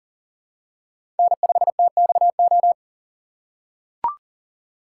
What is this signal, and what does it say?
Morse code, heard as a keyed tone, about 40 wpm, 700 Hz. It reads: N5TXO
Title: USA Call Signs - 5 Characters - 4x Character Spacing - 40wpm